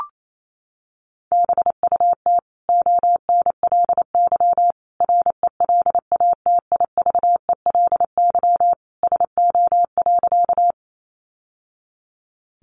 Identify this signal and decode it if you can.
Morse code, heard as a keyed tone, about 28 wpm, 700 Hz. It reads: BUT ONLY RELATIVELY SO.
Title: Sentences from Top 1100 Words 28wpm